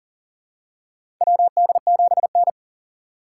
Morse code, heard as a keyed tone, about 40 wpm, 700 Hz. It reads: WD7N
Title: USA Call Signs - 4 Characters - 4x Character Spacing - 40wpm